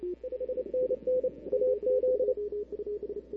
Morse code, heard as a keyed tone, about 27 wpm, 530 Hz. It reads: 5NN AZ
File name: CW / 2005-k5tr arrl cw dx -00:00